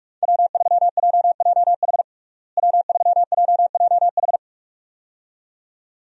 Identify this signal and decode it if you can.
Morse code, heard as a keyed tone, about 47 wpm, 700 Hz. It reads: W3JJH W3JJH
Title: USA Call Signs - 5 Characters - Repeated 2x - 45wpm